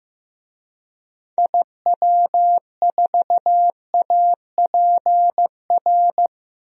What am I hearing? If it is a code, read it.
Morse code, heard as a keyed tone, about 16 wpm, 700 Hz. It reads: IW4APR
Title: DXCC-100 Call Signs - 6 Characters - 15wpm